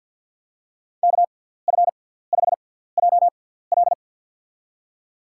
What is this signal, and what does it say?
Morse code, heard as a keyed tone, about 52 wpm, spaced as for 17 wpm, 700 Hz. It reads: KF5JL
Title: USA Call Signs - 5 Characters - 6x Character Spacing - 50wpm